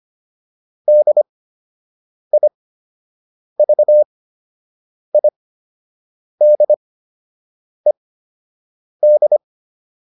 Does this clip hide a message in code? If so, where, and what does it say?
Yes: Morse code from 0.9 to 9.4 s, heard as a keyed tone, about 25 wpm, spaced as for 7 wpm, 600 Hz. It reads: DIVIDED